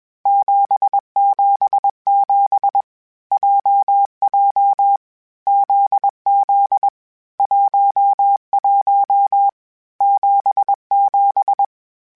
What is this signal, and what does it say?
Morse code, heard as a keyed tone, about 21 wpm, 800 Hz. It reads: 777 JJ ZZ 11 77
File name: Basic Sets / Basic Letter Groups Collective7 / JZ17@20